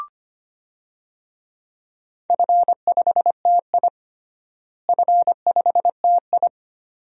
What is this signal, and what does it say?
Morse code, heard as a keyed tone, about 25 wpm, 700 Hz. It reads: F5TI F5TI
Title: DXCC-100 Call Signs - 4 Characters - Repeated 2x - 25wpm